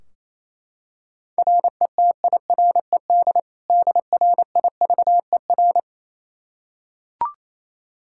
Morse code, heard as a keyed tone, about 27 wpm, 700 Hz. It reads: RETIRED DRIVER
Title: K1USN SST QSO Practice 28wpm